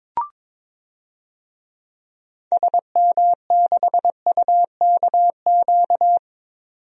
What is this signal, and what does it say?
Morse code, heard as a keyed tone, about 22 wpm, 700 Hz. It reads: SM6UKQ
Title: DXCC-100 Call Signs - 6 Characters - 22wpm